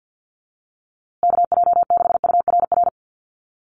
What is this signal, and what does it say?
Morse code, heard as a keyed tone, about 50 wpm, 700 Hz. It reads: KJ6URR
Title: USA Call Signs Repeated 2x 50wpm